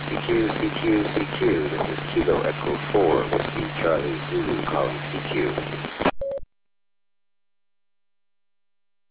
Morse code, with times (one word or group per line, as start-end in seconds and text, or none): none